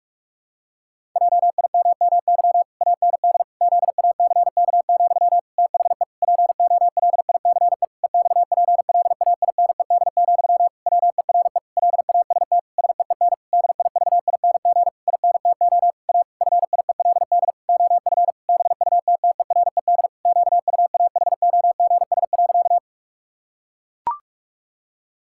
Morse code, heard as a keyed tone, about 41 wpm, 700 Hz. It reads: JIMMY AND ZACK, THE POLICE EXPLAINED, WERE LAST SEEN DIVING INTO A FIELD OF BUTTERED QUAHOGS.
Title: Pangram Challenge v2 - 45wpm